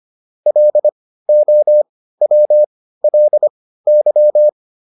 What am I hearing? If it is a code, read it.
Morse code, heard as a keyed tone, about 25 wpm, spaced as for 15 wpm, 600 Hz. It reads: LOWLY